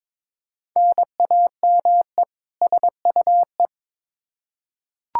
Morse code, heard as a keyed tone, about 22 wpm, 700 Hz. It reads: NAME SUE
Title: Practice QSO - Exchange 1 - Beginning - 22wpm